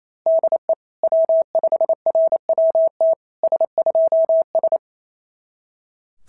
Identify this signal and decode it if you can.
Morse code, heard as a keyed tone, about 25 wpm, 650 Hz. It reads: DE W5RWT S2S
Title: SOTA QSO Practice 28wpm